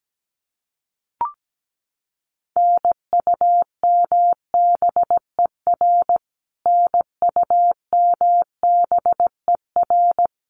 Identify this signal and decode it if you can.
Morse code, heard as a keyed tone, about 17 wpm, 700 Hz. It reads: NUMBER NUMBER
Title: Top 200 Words Repeated 17wpm